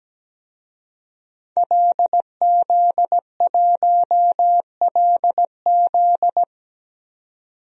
Morse code, heard as a keyed tone, about 17 wpm, 700 Hz. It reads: LZ1LZ
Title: DXCC-100 Call Signs - 5 Characters - 17wpm